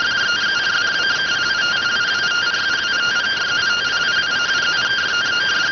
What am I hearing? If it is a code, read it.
Two-tone teleprinter signal, about 170 Hz shift.